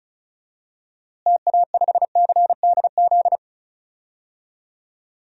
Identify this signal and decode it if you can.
Morse code, heard as a keyed tone, about 36 wpm, 700 Hz. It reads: TA5CDZ
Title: DXCC-100 Call Signs - 6 Characters - Repeated 2x - 35wpm